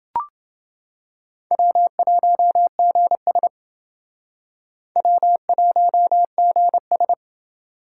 Morse code, heard as a keyed tone, about 29 wpm, 700 Hz. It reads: W1GS W1GS